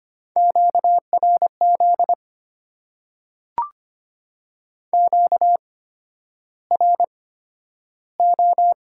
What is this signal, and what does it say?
Morse code, heard as a keyed tone, about 25 wpm, spaced as for 11 wpm, 700 Hz. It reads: QRZ QRO
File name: Top 90 QSO Elements - 8x Character Spacing - 25wpm